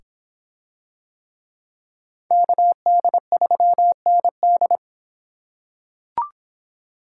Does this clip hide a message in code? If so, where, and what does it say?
Yes: Morse code from 2.3 to 4.8 s, heard as a keyed tone, about 25 wpm, 700 Hz. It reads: KD3ND